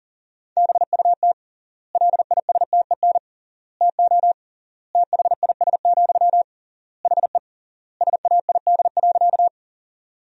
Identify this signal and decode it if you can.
Morse code, heard as a keyed tone, about 37 wpm, 700 Hz. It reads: BUT LISTEN TO THIS, HE SAID.
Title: 3x Word-Spacing - Sentences from Top 400 Words - 40wpm